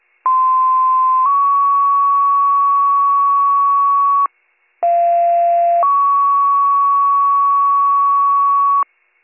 Two-tone page, heard as a one-second tone, then a three-second tone, about 1040 then 1120 Hz, about 690 then 1120 Hz, with noise. It is heard from 0.3 to 8.8 s.